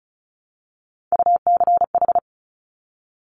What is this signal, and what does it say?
Morse code, heard as a keyed tone, about 36 wpm, 700 Hz. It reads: UCH
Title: Top Three Letter Combinations in Words 35wpm